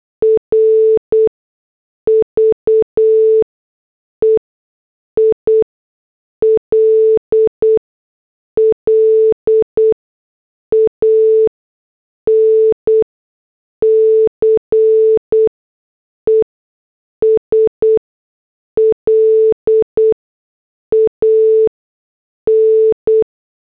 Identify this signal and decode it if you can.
Morse code, heard as a keyed tone, about 8 wpm, spaced as for 6 wpm, 440 Hz. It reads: RVEILLANCE SLAN
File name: morse / headlines